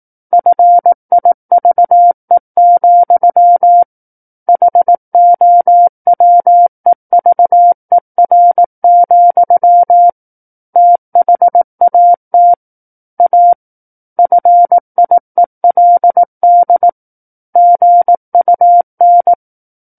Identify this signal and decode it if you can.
Morse code, heard as a keyed tone, about 18 wpm, 700 Hz. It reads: FIVE, HOWEVER, THAT A FIELD GUN
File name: 18 WPM